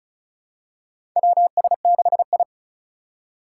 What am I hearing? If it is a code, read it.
Morse code, heard as a keyed tone, about 36 wpm, 700 Hz. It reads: WS6I
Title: USA Call Signs - 4 Characters - 2x Character Spacing - 35wpm